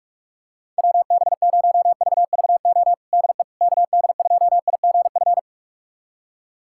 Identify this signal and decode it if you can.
Morse code, heard as a keyed tone, about 46 wpm, 700 Hz. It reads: WB0VVO DE KD2IGF
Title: USA Callsign DE Callsign - 6 Characters - 45wpm